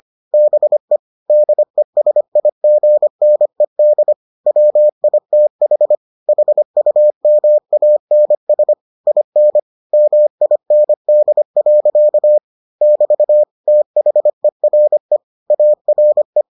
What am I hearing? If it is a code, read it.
Morse code, heard as a keyed tone, about 25 wpm, 600 Hz. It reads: BE DESIGNED WITH HUMANS IN MIND. <BT> THERE ARE